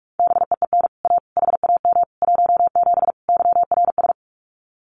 Morse code, heard as a keyed tone, about 44 wpm, 700 Hz. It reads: BEEN A HAM 17 YRS
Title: Common QSO Phrases Encore 45wpm